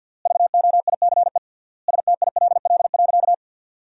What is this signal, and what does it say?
Morse code, heard as a keyed tone, about 50 wpm, 700 Hz. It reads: VOICE STILL.
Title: 3x Word-Spacing - Sentences from Top 500 Words - 50wpm